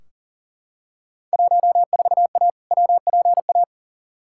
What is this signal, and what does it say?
Morse code, heard as a keyed tone, about 40 wpm, 700 Hz. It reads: 14A WPA
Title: Field Day Exchanges Encore - 40wpm